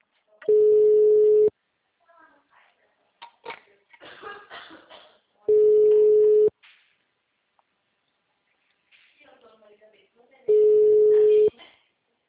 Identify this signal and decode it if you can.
Ringback tone (Europe, 425 Hz), with speech.